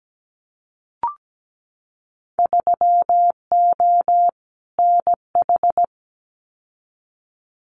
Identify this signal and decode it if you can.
Morse code, heard as a keyed tone, about 17 wpm, 700 Hz. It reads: 3O NH
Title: Winter Field Day Exchanges Encore 17wpm